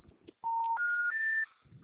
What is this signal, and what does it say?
Special information tone.
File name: out- -2007- - - .367